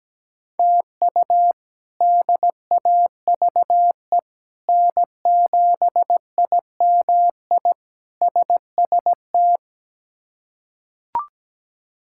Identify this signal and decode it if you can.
Morse code, heard as a keyed tone, about 17 wpm, 700 Hz. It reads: TU DAVE N7IMI SST
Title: K1USN SST QSO Practice - 17wpm